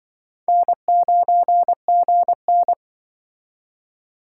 Morse code, heard as a keyed tone, about 24 wpm, 700 Hz. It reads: N9GN
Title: Speed Racing - USA 1x2 Call Signs 20wpm